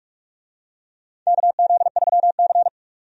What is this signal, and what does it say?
Morse code, heard as a keyed tone, about 46 wpm, 700 Hz. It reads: KZ3C
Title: USA Call Signs - 4 Characters - 45wpm